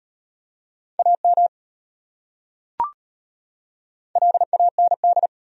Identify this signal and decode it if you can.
Morse code, heard as a keyed tone, about 31 wpm, 700 Hz. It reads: AM LAND
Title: Speed-Racing Top 500 Words 25wpm